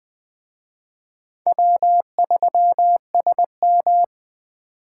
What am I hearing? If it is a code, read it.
Morse code, heard as a keyed tone, about 23 wpm, 700 Hz. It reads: W3SM
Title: Speed Racing - USA Call Signs 20wpm